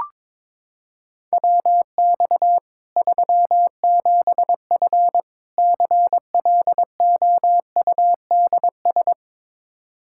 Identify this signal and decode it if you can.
Morse code, heard as a keyed tone, about 22 wpm, 700 Hz. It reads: WX 37F CLOUDS
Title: Practice QSO - Exchange 2 - Info - 22wpm